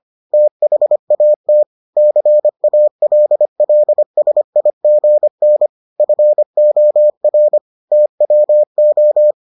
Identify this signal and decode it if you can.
Morse code, heard as a keyed tone, about 25 wpm, 600 Hz. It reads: THAT CALLSIGN FOR TWO